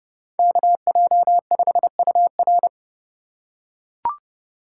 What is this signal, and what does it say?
Morse code, heard as a keyed tone, about 29 wpm, 700 Hz. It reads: KJ5UR